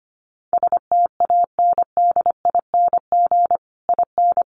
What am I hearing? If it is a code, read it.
Morse code, heard as a keyed tone, about 25 wpm, 700 Hz. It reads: STANDING IN
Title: Sets of 5 Words Encore2 25wpm